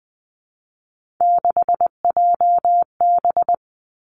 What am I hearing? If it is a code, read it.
Morse code, heard as a keyed tone, about 20 wpm, 700 Hz. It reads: 6JB
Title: Three Challenging Letter-Number Combinations 20wpm